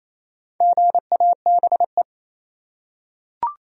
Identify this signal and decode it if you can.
Morse code, heard as a keyed tone, about 28 wpm, 700 Hz. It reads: GABE
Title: USA Common Names - Max 5 Letters - 28wpm